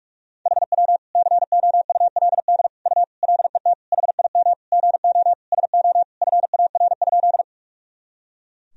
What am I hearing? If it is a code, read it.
Morse code, heard as a keyed tone, about 41 wpm, 700 Hz. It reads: HW COULD U LET HIM GO SO FAR?
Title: Sentences from Top 300 Words and Abbreviations 45wpm